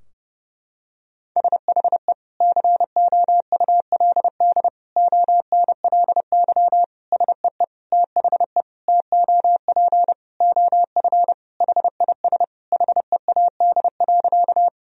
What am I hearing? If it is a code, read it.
Morse code, heard as a keyed tone, about 30 wpm, 700 Hz. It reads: SHE COULD ONLY SEE THE TOP OF HIS HEAD.
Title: Sentences from Top 1000 Words - Review Part 2 - 30wpm